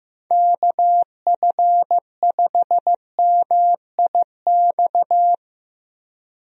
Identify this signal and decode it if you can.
Morse code, heard as a keyed tone, about 15 wpm, 700 Hz. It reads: KF5MIX
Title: USA Call Signs - 6 Characters - Repeated 3x - 15wpm